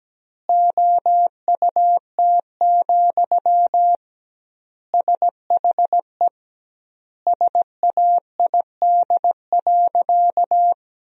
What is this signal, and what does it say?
Morse code, heard as a keyed tone, about 17 wpm, 700 Hz. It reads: OUT, SHE SAID.
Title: 2x Word-Spacing - Sentences from Top 500 Words - 17wpm